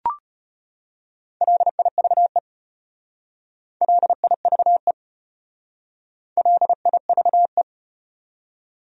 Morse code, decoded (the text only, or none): LIVE LIVE LIVE